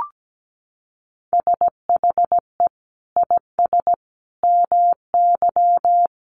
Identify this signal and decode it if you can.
Morse code, heard as a keyed tone, about 17 wpm, 700 Hz. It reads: SHE IS MY
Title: Sentences from Top 300 Words 17wpm